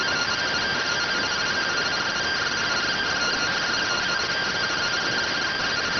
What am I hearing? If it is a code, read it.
Two-tone teleprinter signal, about 170 Hz shift.